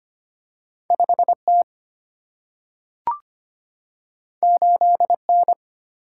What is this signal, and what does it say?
Morse code, heard as a keyed tone, about 25 wpm, 700 Hz. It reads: 5 T 8 N
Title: Single-Letter Single-Number 25wpm